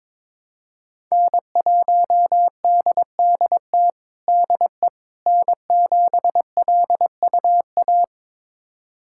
Morse code, heard as a keyed tone, about 22 wpm, 700 Hz. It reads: N1DDT DE N7LUA
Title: USA Callsign DE Callsign - 5 Characters - 22wpm